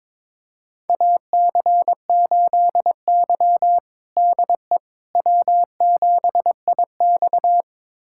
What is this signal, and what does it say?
Morse code, heard as a keyed tone, about 22 wpm, 700 Hz. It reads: AC8Y DE W7IX
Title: USA Callsign DE Callsign - 4 Characters - 22wpm